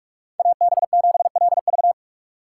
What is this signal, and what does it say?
Morse code, heard as a keyed tone, about 46 wpm, 700 Hz. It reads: AB7LV
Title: USA Call Signs - 5 Characters - 3x Character Spacing - 45wpm